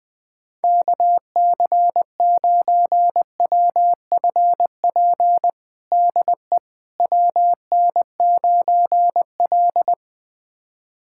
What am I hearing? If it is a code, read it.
Morse code, heard as a keyed tone, about 20 wpm, 700 Hz. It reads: KC9WFP DE WN9L
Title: USA Callsign DE Callsign 20wpm